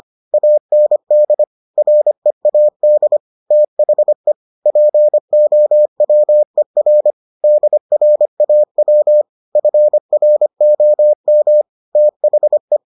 Morse code, heard as a keyed tone, about 25 wpm, 600 Hz. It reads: AND READ THE POWER DRAW FROM THE